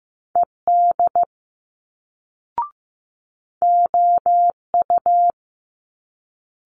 Morse code, heard as a keyed tone, about 15 wpm, 700 Hz. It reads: ED OU
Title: Multi-letter Phonemes Quick Fire 15wpm